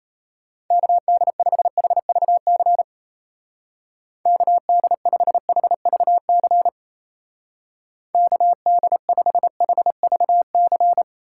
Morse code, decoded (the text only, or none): KD5HVC KD5HVC KD5HVC